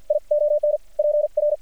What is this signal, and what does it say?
Morse code, heard as a keyed tone, about 48 wpm, 600 Hz. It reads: I ON PR
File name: archived / CRSnet